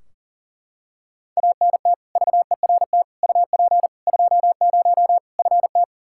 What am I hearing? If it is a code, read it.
Morse code, heard as a keyed tone, about 37 wpm, 700 Hz. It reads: ANT VERT UP 20 FT
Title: K1USN SST QSO Practice 40wpm